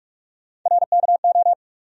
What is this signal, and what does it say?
Morse code, heard as a keyed tone, about 44 wpm, 700 Hz. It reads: RKO
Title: USA Call Sign Postfixes - 3 Letters 45wpm